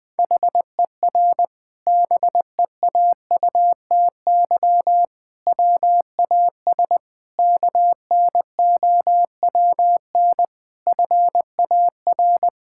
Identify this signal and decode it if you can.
Morse code, heard as a keyed tone, about 21 wpm, 700 Hz. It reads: HER BEAUTY WAS KNOWN FAR
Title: Sentences from Top 1000 Words - Review Part 6 - 20wpm